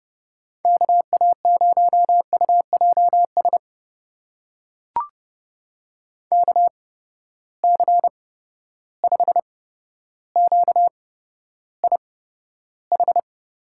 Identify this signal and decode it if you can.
Morse code, heard as a keyed tone, about 31 wpm, spaced as for 13 wpm, 700 Hz. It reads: KA0UJS KC5QIH
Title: USA Call Signs - 6 Characters - 8x Character Spacing - 30wpm